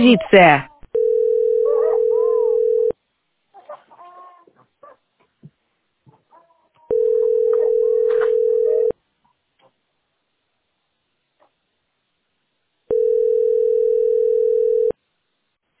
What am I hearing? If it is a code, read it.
Ringback tone (North America), 440 + 480 Hz, with speech.
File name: 26fa9a75-0c25-44dd-b0f8